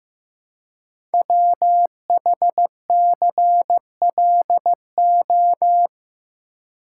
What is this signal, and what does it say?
Morse code, heard as a keyed tone, about 15 wpm, 700 Hz. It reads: WHCLO
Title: Code Groups - 5 Letter:Numbers - 15wpm